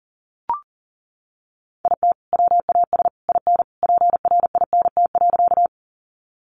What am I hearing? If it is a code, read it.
Morse code, heard as a keyed tone, about 40 wpm, 700 Hz. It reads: IT WAS IN PRINT.
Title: Sentences from Top 1400 Words 40wpm